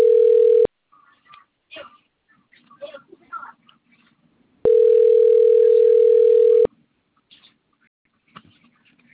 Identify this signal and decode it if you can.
Ringback tone (North America), 440 + 480 Hz.
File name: rg-1600-unknown- - - .43929